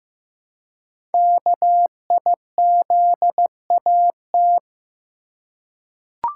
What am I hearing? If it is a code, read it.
Morse code, heard as a keyed tone, about 15 wpm, 700 Hz. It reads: KIZAT
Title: Code Groups - 5 Letters - 15wpm